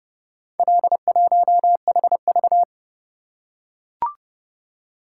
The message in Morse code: L1HV

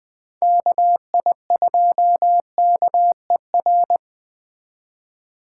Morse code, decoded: KI2KER

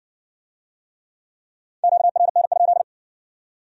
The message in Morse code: QRN?

